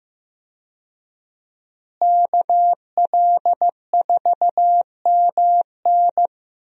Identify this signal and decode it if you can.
Morse code, heard as a keyed tone, about 16 wpm, 700 Hz. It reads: KL4MN
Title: USA Call Signs - 5 Characters - Repeated 2x - 15wpm